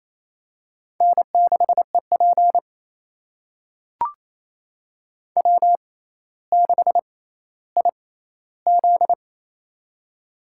Morse code, heard as a keyed tone, about 28 wpm, spaced as for 14 wpm, 700 Hz. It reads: N6EP W6IZ
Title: USA Call Signs - 4 Characters - 6x Character Spacing - 28wpm